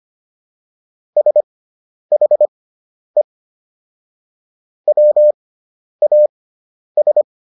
Morse code, heard as a keyed tone, about 25 wpm, spaced as for 10 wpm, 600 Hz. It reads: SHE WAS